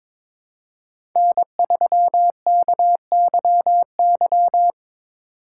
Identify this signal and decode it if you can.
Morse code, heard as a keyed tone, about 22 wpm, 700 Hz. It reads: N3KYY
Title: USA Call Signs - 5 Characters - 4x Character Spacing - 22wpm